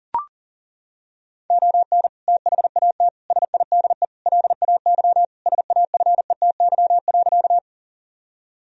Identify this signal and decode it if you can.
Morse code, heard as a keyed tone, about 40 wpm, 700 Hz. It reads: ON THAT SIDE LAY SAFETY.
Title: Sentences from Top 1000 Words - Review Part 4 - 40wpm